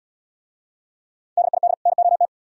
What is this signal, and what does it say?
Morse code, heard as a keyed tone, about 72 wpm, 700 Hz. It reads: DID NOT